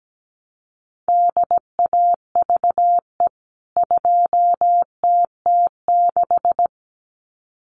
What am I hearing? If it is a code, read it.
Morse code, heard as a keyed tone, about 17 wpm, 700 Hz. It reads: DAVE 2TT6
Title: CWT Exchange 17wpm